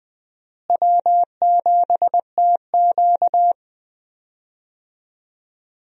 Morse code, heard as a keyed tone, about 20 wpm, 700 Hz. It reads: W7TQ
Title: USA Call Signs - 4 Characters - Repeated 3x - 20wpm